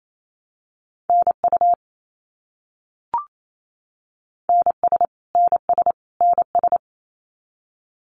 Morse code, heard as a keyed tone, about 28 wpm, 700 Hz. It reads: NU NS NS NS
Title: Canadian Provinces and Territories - Repeated 3x - 28wpm